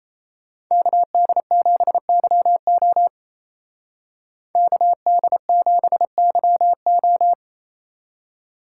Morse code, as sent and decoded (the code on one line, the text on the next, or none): -.- -.. --... -.-- --- / -.- -.. --... -.-- ---
KD7YO KD7YO